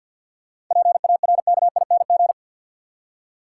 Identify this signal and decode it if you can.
Morse code, heard as a keyed tone, about 52 wpm, 700 Hz. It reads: PARKING
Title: Seven Letter Words - 50wpm